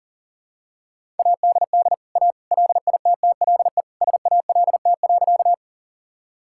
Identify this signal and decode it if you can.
Morse code, heard as a keyed tone, about 40 wpm, 700 Hz. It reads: ADD A LITTLE SALT.
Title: Sentences from Top 1000 Words - Review Part 5 - 40wpm